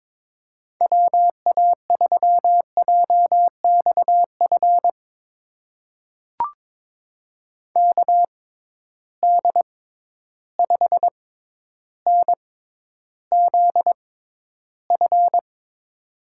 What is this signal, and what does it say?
Morse code, heard as a keyed tone, about 22 wpm, spaced as for 11 wpm, 700 Hz. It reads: WA3JXF KD5NZF
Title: USA Call Signs - 6 Characters - 6x Character Spacing - 22wpm